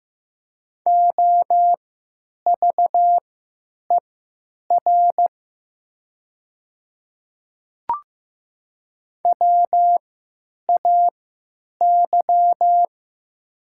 Morse code, heard as a keyed tone, about 15 wpm, spaced as for 9 wpm, 700 Hz. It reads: OVER WAY
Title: 3x Character Spacing - Top 100 Words - 15wpm